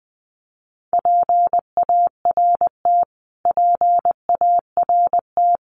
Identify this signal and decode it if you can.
Morse code, heard as a keyed tone, about 19 wpm, 700 Hz. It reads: PART PART
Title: Top 500 Words Repeated 20wpm